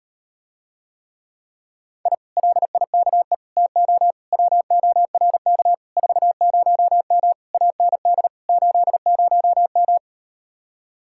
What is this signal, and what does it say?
Morse code, heard as a keyed tone, about 38 wpm, 700 Hz. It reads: I LIKE TO WORK 40M AND 80M